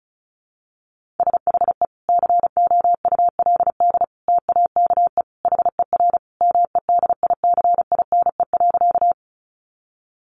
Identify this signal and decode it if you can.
Morse code, heard as a keyed tone, about 34 wpm, 700 Hz. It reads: SHE COULD TAKE HER MEDICINE.
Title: Sentences from Top 900 Words 35wpm